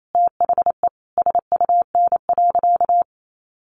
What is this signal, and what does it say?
Morse code, heard as a keyed tone, about 28 wpm, 700 Hz. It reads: THE SUN.
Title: Sentences from Top 500 Words 28wpm